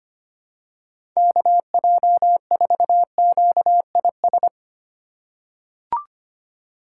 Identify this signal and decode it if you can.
Morse code, heard as a keyed tone, about 25 wpm, 700 Hz. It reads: KJ4QIS
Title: USA Call Signs - 6 Characters - 2x Character Spacing - 25wpm